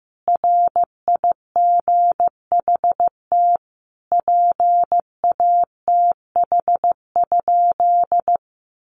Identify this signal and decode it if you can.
Morse code, heard as a keyed tone, about 15 wpm, 700 Hz. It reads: RIGHT PATH?
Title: Sentences from Top 1400 Words 15wpm